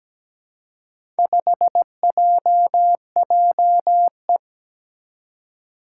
Morse code, heard as a keyed tone, about 17 wpm, 700 Hz. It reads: 5JJE
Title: Four Challenging Letter-Number Combinations 17wpm